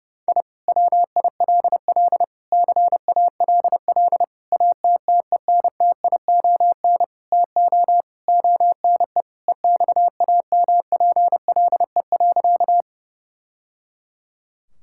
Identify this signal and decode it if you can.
Morse code, heard as a keyed tone, about 30 wpm, 700 Hz. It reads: I WILL CALL ATTENTION TO ONE EXAMPLE.
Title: Sentences from Top 1000 Words - Review Part 6 - 30wpm